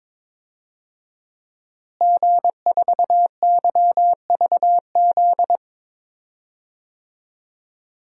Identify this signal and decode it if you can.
Morse code, heard as a keyed tone, about 21 wpm, 700 Hz. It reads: G4YVZ
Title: DXCC-100 Call Signs - 5 Characters - Repeated 2x - 22wpm